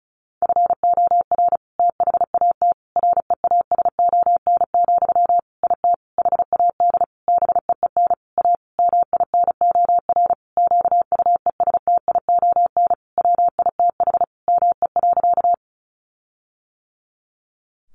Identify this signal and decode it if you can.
Morse code, heard as a keyed tone, about 34 wpm, 700 Hz. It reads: FOR THAT REASON, IT HAD BEEN A MINOR QUESTION WITH ME.
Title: Sentences from Top 1500 Words 35wpm